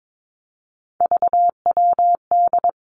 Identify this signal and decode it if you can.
Morse code, heard as a keyed tone, about 22 wpm, 700 Hz. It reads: VWD
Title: Three Challenging Letter-Number Combinations 22wpm